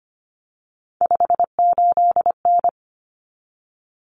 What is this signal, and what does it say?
Morse code, heard as a keyed tone, about 25 wpm, 700 Hz. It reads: 58N
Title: Common Cut Numbers - 3-Digit 25wpm